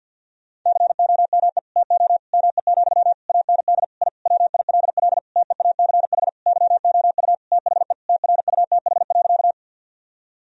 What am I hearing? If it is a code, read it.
Morse code, heard as a keyed tone, about 50 wpm, 700 Hz. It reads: COME TO ME, AND I WILL TEACH YOU THE TRUTH.
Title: Sentences from Top 1000 Words - Review Part 2 - 50wpm